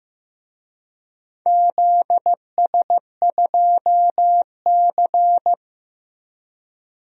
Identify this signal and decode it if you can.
Morse code, heard as a keyed tone, about 15 wpm, 700 Hz. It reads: ZS2C
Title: DXCC-100 Call Signs - 4 Characters - 15wpm